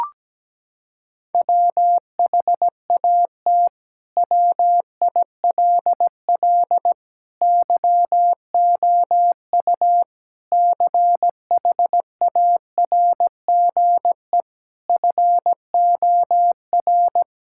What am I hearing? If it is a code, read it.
Morse code, heard as a keyed tone, about 17 wpm, 700 Hz. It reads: WHAT WILL YOU CHARGE FOR
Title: Sentences from Top 1000 Words - Review Part 6 - 17wpm